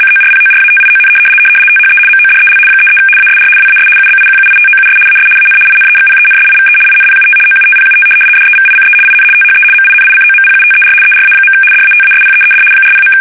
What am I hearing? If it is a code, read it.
RTTY, about 75 baud. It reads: CORPORATION THE MODEL 33 ASR WAS UBIQUITOUS AS AN INEXPENSIVE INPUT OUTPUT DEVICE IN THE MINICOMPUTER ERA. DATASPEED 40.
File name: fsk850-75B / teletype-corp